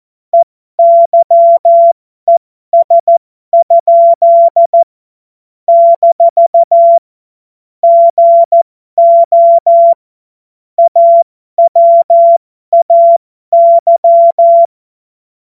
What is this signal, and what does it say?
Morse code, heard as a keyed tone, about 14 wpm, spaced as for 12 wpm, 680 Hz. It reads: EYES? - GO AWAY
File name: Morsekurs / Witze